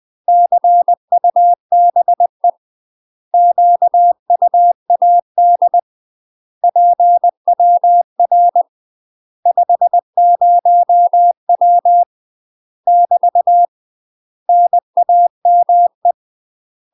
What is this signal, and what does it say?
Morse code, heard as a keyed tone, about 20 wpm, 700 Hz. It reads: CUBE QUAD PWR 50W <BT> NAME
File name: UK MASTER QSOs TRAINING / 20WPM / UK MASTER QSOs 0060